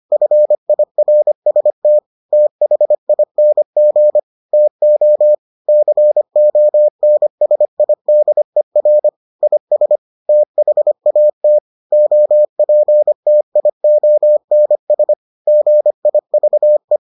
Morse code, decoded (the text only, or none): FIRST THING TO CONSIDER IS THAT OPTIONS GIVE